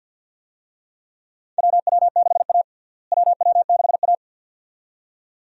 Morse code, heard as a keyed tone, about 52 wpm, 700 Hz. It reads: WW6A WW6A
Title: USA Call Signs - 4 Characters - Repeated 2x - 50wpm